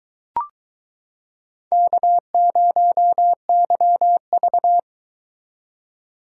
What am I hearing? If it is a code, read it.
Morse code, heard as a keyed tone, about 23 wpm, 700 Hz. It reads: K0YV